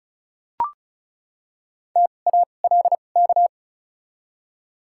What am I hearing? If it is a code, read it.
Morse code, heard as a keyed tone, about 36 wpm, spaced as for 25 wpm, 700 Hz. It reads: TALK
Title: 2x Character Spacing - Top 500 Words - 35wpm